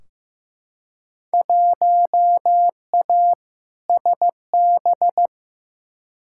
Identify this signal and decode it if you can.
Morse code, heard as a keyed tone, about 15 wpm, 700 Hz. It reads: 1A SB
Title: Field Day Exchanges Encore - 15wpm